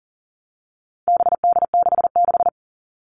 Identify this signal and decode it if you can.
Morse code, heard as a keyed tone, about 40 wpm, 700 Hz. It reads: BD66
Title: All Cut Numbers - 4-Digit 40wpm